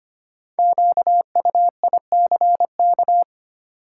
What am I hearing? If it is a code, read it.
Morse code, heard as a keyed tone, about 25 wpm, 700 Hz. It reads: QUICK
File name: 5 Letter Words - First 3 Letters Given - 25wpm